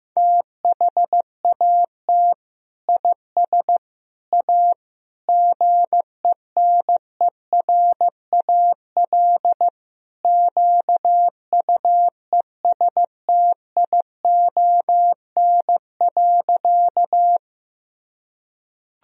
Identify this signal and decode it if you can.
Morse code, heard as a keyed tone, about 15 wpm, 700 Hz. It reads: THAT IS A GENERAL QUESTION.
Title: Sentences from Top 1000 Words - Review Part 3 - 15wpm